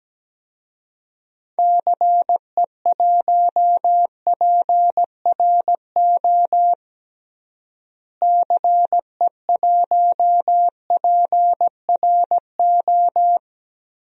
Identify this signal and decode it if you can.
Morse code, heard as a keyed tone, about 17 wpm, 700 Hz. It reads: CE1PRO CE1PRO
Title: DXCC-100 Call Signs - 6 Characters - Repeated 2x - 17wpm